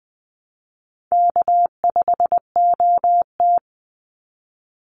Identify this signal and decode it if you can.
Morse code, heard as a keyed tone, about 20 wpm, 700 Hz. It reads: K5OT
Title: Easy Call Signs Repeated 2x - 20wpm